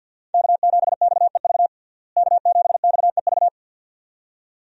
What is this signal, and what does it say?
Morse code, heard as a keyed tone, about 53 wpm, 700 Hz. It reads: K7XEV K7XEV
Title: USA Call Signs - 5 Characters - Repeated 2x - 50wpm